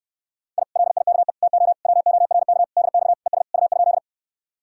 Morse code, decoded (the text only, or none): I LIKE TO WORK DX ES U?